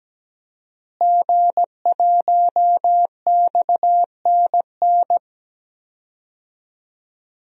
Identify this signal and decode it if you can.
Morse code, heard as a keyed tone, about 17 wpm, 700 Hz. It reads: G1XNN
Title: DXCC-100 Call Signs - 5 Characters - Repeated 2x - 17wpm